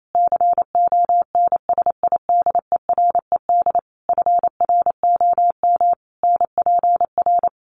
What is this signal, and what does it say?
Morse code, heard as a keyed tone, about 28 wpm, 700 Hz. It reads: CONSIDERED FROM NPR
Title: Sets of 5 Words Encore 28wpm